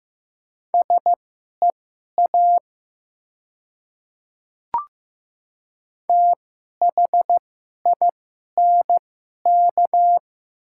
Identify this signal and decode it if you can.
Morse code, heard as a keyed tone, about 15 wpm, spaced as for 11 wpm, 700 Hz. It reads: SEA THINK
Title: 2x Character Spacing - Top 500 Words - 15wpm